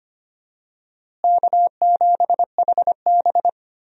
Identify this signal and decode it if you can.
Morse code, heard as a keyed tone, about 25 wpm, 700 Hz. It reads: K7HB
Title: USA Call Signs - 4 Characters - Repeated 3x - 25wpm